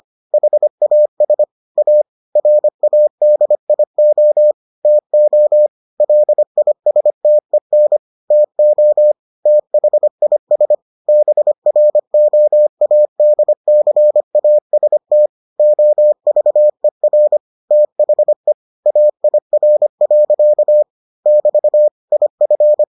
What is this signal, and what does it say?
Morse code, heard as a keyed tone, about 25 wpm, 600 Hz. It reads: HAS A RADIO TO LISTEN TO THIS BROADCAST OVER THE AIR. <BT> IF